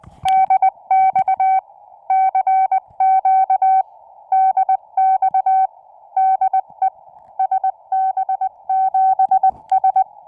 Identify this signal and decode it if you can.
Morse code, heard as a keyed tone, about 20 wpm, 760 Hz. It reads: DX CQ DX DE SB7S